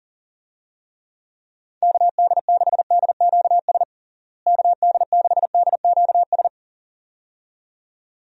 Morse code, heard as a keyed tone, about 40 wpm, 700 Hz. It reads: KD6DQS KD6DQS
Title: USA Call Signs - 6 Characters - Repeated 2x - 40wpm